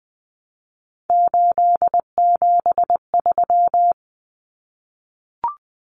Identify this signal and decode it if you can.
Morse code, heard as a keyed tone, about 20 wpm, 700 Hz. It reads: 873
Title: Common Cut Numbers - 3-Digit 20wpm